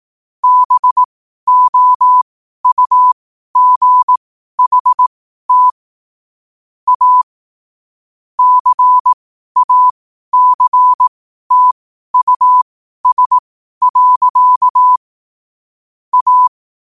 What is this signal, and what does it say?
Morse code, heard as a keyed tone, about 18 wpm, spaced as for 12 wpm, 1000 Hz. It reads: BOUGHT A CACTUS. A
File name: qotdfiles / 12wpm